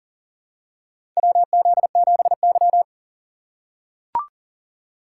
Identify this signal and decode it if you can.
Morse code, heard as a keyed tone, about 38 wpm, 700 Hz. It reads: WZ7Y
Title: USA Call Signs - 4 Characters - 3x Character Spacing - 40wpm